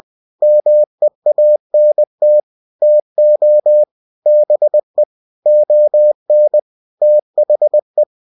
Morse code, heard as a keyed tone, about 20 wpm, 600 Hz. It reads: MEANT TO BE ON THE
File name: cw-books / winnie-the-pooh / winnie-20wpm-page-0095